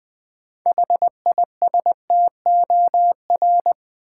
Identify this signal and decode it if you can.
Morse code, heard as a keyed tone, about 20 wpm, 700 Hz. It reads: HISTOR
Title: Build A Word - 3 Letters At A Time - 8-Letter Words - 20wpm